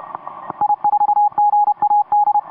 Morse code, heard as a keyed tone, about 29 wpm, 830 Hz. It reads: I4GAD
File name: 17m cw